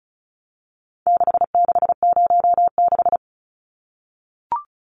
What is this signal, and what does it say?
Morse code, heard as a keyed tone, about 34 wpm, 700 Hz. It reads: 6606
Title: Numbers - 4-Digit 35wpm